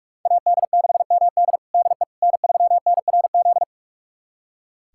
Morse code, heard as a keyed tone, about 42 wpm, 700 Hz. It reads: AD6MD DE N3NRZ
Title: USA Callsign DE Callsign - 5 Characters - 45wpm